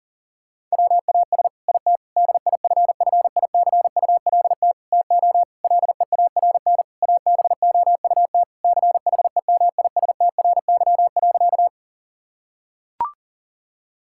Morse code, heard as a keyed tone, about 40 wpm, 700 Hz. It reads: WAS IT DIFFICULT TO LEARN ABOUT CHEMISTRY.
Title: Sentences from Top 1000 Words - Review Part 6 - 40wpm